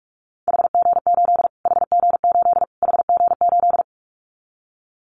Morse code, heard as a keyed tone, about 44 wpm, 700 Hz. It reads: HG8 HG8 HG8
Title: Callsign Prefixes for Top 25 DXCC - Repeated 3x - 45wpm